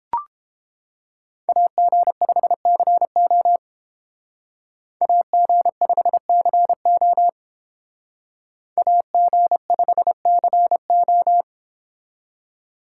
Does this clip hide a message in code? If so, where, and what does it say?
Yes: Morse code from 1.5 to 11.4 s, heard as a keyed tone, about 28 wpm, 700 Hz. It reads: AG5CO AG5CO AG5CO